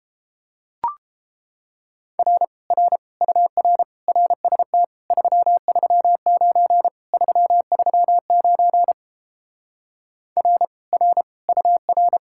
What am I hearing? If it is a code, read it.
Morse code, heard as a keyed tone, about 29 wpm, 700 Hz. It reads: R R UR RST 339 339 R R UR